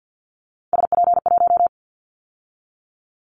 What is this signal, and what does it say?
Morse code, heard as a keyed tone, about 50 wpm, 700 Hz. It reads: SP1
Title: Callsign Prefixes for Top 25 DXCC - 50wpm